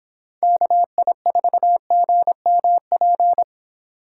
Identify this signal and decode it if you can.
Morse code, heard as a keyed tone, about 26 wpm, 700 Hz. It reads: KI4GMP
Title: Speed Racing - USA Call Signs 22wpm